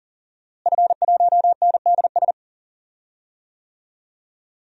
Morse code, heard as a keyed tone, about 40 wpm, 700 Hz. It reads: F1NDS
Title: DXCC-100 Call Signs - 5 Characters - Repeated 3x - 40wpm